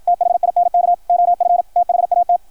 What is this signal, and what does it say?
Morse code, heard as a keyed tone, about 53 wpm, 700 Hz, with radio noise. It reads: THINK OF THAT
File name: archived / CRSnet